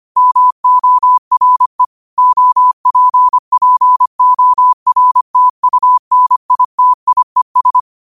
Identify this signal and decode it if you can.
Morse code, heard as a keyed tone, about 25 wpm, 1000 Hz. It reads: MORE OPPORTUNITIES
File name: qotdfiles / 25wpm